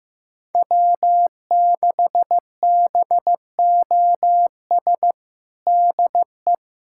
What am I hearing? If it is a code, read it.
Morse code, heard as a keyed tone, about 15 wpm, 700 Hz. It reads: W6BOS DE
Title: USA Callsign DE Callsign - 5 Characters - 15wpm